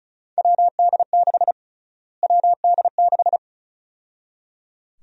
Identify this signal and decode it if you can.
Morse code, heard as a keyed tone, about 36 wpm, 700 Hz. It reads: WD6 WD6
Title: USA Call Sign Prefixes - Repeated 2x - 35wpm